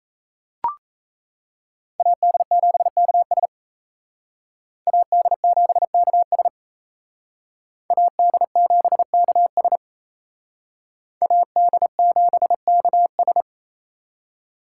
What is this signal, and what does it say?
Morse code, heard as a keyed tone, about 35 wpm, 700 Hz. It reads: AD7KS AD7KS AD7KS AD7KS
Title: Speed Racing - USA 2x2 Call Signs 28wpm